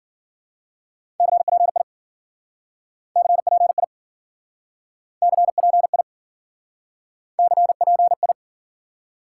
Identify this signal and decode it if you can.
Morse code, heard as a keyed tone, about 52 wpm, 700 Hz. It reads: CPI CPI CPI CPI